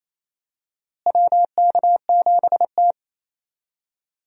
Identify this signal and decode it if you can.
Morse code, heard as a keyed tone, about 28 wpm, 700 Hz. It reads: WK7T
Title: USA Call Signs - 4 Characters - Repeated 2x - 28wpm